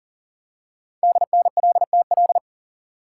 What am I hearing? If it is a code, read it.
Morse code, heard as a keyed tone, about 39 wpm, 700 Hz. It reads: DNLTL
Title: Code Groups - 5 Letters - 40wpm